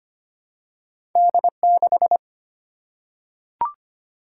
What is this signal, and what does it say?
Morse code, heard as a keyed tone, about 25 wpm, 690 Hz. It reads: D6
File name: Two Letter-Number Combinations - 25wpm